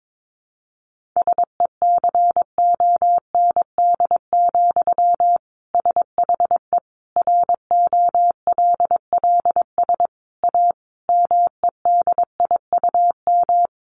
Morse code, heard as a keyed tone, about 22 wpm, 700 Hz. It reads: SECOND, SHE ROLLS A MEDIUM
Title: Easy Phrases 22wpm